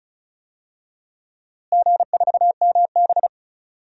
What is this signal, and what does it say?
Morse code, heard as a keyed tone, about 36 wpm, 700 Hz. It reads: G4MB
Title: DXCC-100 Call Signs - 4 Characters - Repeated 2x - 35wpm